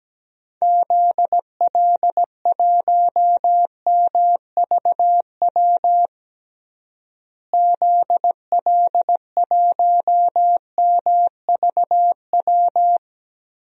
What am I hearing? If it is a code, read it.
Morse code, heard as a keyed tone, about 17 wpm, 700 Hz. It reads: ZL1MVW ZL1MVW
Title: DXCC-100 Call Signs - 6 Characters - Repeated 3x - 17wpm